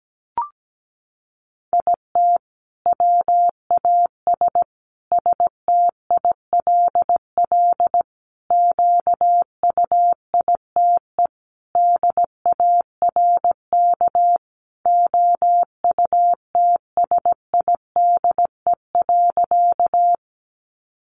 Morse code, heard as a keyed tone, about 17 wpm, 700 Hz. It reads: IT WAS STILL QUITE DARK OUTSIDE.
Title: Sentences from Top 900 Words 17wpm